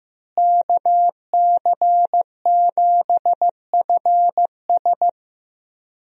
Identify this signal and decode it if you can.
Morse code, heard as a keyed tone, about 16 wpm, 700 Hz. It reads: KC7FS
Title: USA Call Signs - 5 Characters - Repeated 3x - 15wpm